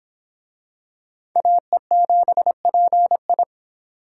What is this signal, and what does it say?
Morse code, heard as a keyed tone, about 24 wpm, 700 Hz. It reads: AE7PI